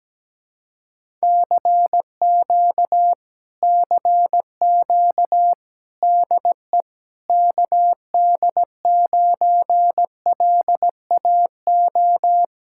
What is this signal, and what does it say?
Morse code, heard as a keyed tone, about 17 wpm, 700 Hz. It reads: CQ CQ DE KD9LAO
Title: Speed-Racing Common QSO Phrases - 17wpm